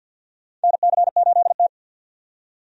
Morse code, heard as a keyed tone, about 50 wpm, 700 Hz. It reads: NC8T
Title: USA Call Signs - 4 Characters - 50wpm